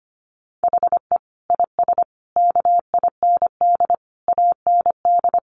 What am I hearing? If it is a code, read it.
Morse code, heard as a keyed tone, about 25 wpm, 700 Hz. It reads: HE IS KIND AND
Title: Sentences from Top 1400 Words 25wpm